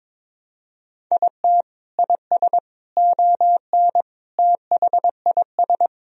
Morse code, heard as a keyed tone, about 21 wpm, 700 Hz. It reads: IT IS ON THIS
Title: Sentences from Top 1000 Words - Review Part 1 - 22wpm